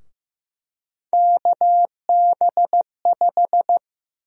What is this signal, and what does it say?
Morse code, heard as a keyed tone, about 15 wpm, 700 Hz. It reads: KB5
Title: USA Call Sign Prefixes - 15wpm